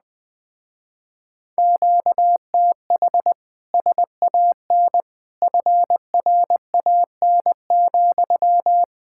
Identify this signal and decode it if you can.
Morse code, heard as a keyed tone, about 20 wpm, 700 Hz. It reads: QTH SAN FRAN,
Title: Speed-Racing Common QSO Phrases - 17wpm